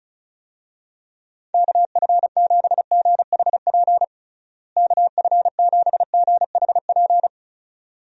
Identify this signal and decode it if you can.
Morse code, heard as a keyed tone, about 36 wpm, 700 Hz. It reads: KF7GHP KF7GHP
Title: USA Call Signs - 6 Characters - Repeated 2x - 35wpm